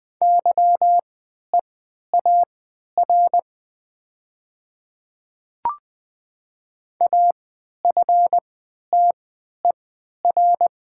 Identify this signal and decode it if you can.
Morse code, heard as a keyed tone, about 20 wpm, spaced as for 11 wpm, 700 Hz. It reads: YEAR AFTER
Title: 3x Character Spacing - Top 200 Words - 20wpm